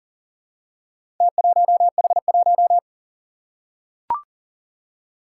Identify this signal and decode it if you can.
Morse code, heard as a keyed tone, about 40 wpm, 700 Hz. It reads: T1H1
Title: Four Challenging Letter-Number Combinations 40wpm